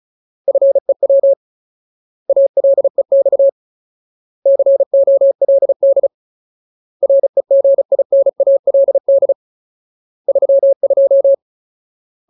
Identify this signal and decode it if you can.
Morse code, heard as a keyed tone, about 34 wpm, 550 Hz. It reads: FEW ALEX COLD REGINALD 32
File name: Ultimate Word List / 35WPM Ultimate